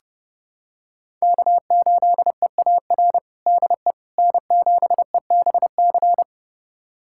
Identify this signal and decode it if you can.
Morse code, heard as a keyed tone, about 30 wpm, 700 Hz. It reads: K8EAR DE N7EBC
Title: USA Callsign DE Callsign - 5 Characters - 30wpm